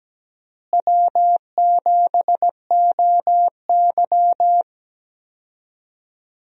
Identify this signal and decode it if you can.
Morse code, heard as a keyed tone, about 17 wpm, 700 Hz. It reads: W7OY
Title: USA Call Signs - 4 Characters - Repeated 3x - 17wpm